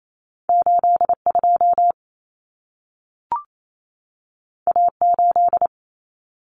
Morse code, heard as a keyed tone, about 28 wpm, 700 Hz. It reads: 82 A8
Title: Common Cut Numbers - 2-Digit 28wpm